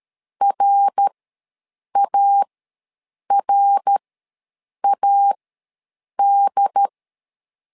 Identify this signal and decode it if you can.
Morse code, heard as a keyed tone, about 13 wpm, spaced as for 7 wpm, 800 Hz. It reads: RARAD
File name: LEZIONI / 3 MWRO / 3 MWR ws 12wpm 1809